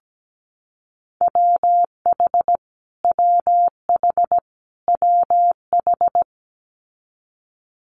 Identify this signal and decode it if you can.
Morse code, heard as a keyed tone, about 17 wpm, 700 Hz. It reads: WH WH WH
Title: Multi-letter Phonemes Repeated 3x 17wpm